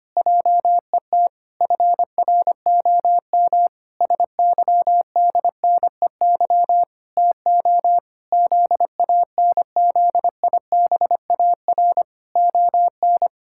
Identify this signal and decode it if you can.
Morse code, heard as a keyed tone, about 25 wpm, 700 Hz. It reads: JET FROM SYDNEY TO ZANZIBAR ON
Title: Pangram Challenge v2 - 25wpm